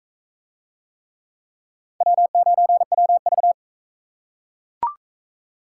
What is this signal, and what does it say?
Morse code, heard as a keyed tone, about 40 wpm, 700 Hz. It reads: W9WV